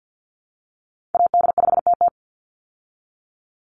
Morse code, heard as a keyed tone, about 49 wpm, 700 Hz. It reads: AD5TT
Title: Call Signs Repeated 3x - 50wpm